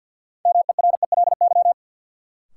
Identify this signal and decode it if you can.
Morse code, heard as a keyed tone, about 51 wpm, 700 Hz. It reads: MERELY
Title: Sentences from Top 1400 Words - New Word Drill - 50wpm